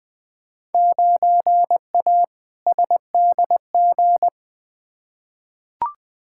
Morse code, heard as a keyed tone, about 20 wpm, 700 Hz. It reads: 9A SDG
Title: Field Day Exchanges Encore - 20wpm